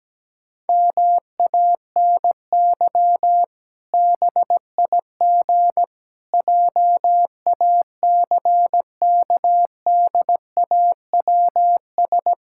Morse code, heard as a keyed tone, about 17 wpm, 700 Hz. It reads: MANY BIG JACKDAWS
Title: Pangram Challenge v2 - 17wpm